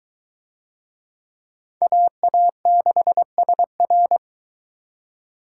Morse code, heard as a keyed tone, about 21 wpm, 700 Hz. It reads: AA6SR